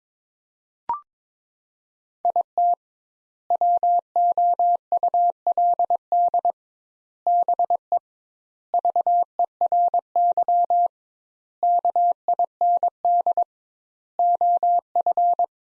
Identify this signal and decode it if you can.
Morse code, heard as a keyed tone, about 22 wpm, 700 Hz. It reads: IT WOULD BE VERY KIND OF
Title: Sentences Top 200 Words - 22wpm